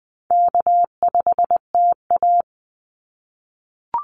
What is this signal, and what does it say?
Morse code, heard as a keyed tone, about 20 wpm, 700 Hz. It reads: K5TA
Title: Easy Call Signs - 20wpm